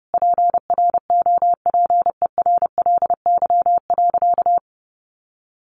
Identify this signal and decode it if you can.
Morse code, heard as a keyed tone, about 30 wpm, 700 Hz. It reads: PROPERLY.
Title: Sentences from Top 1100 Words 30wpm